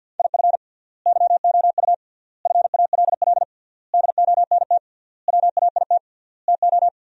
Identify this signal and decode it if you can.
Morse code, heard as a keyed tone, about 51 wpm, 700 Hz. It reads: IF YOU FALL DONT WAIT TO
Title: 3x Word-Spacing - Sentences from Top 500 Words - 50wpm